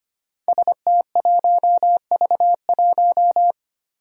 Morse code, heard as a keyed tone, about 25 wpm, 700 Hz. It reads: ST1V1
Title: Five Challenging Letter-Number Combinations 25wpm